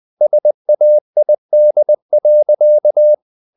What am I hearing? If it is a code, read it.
Morse code, heard as a keyed tone, about 20 wpm, 600 Hz. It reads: SAID.
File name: the old man and the sea / the old man and the sea part 1 20wpm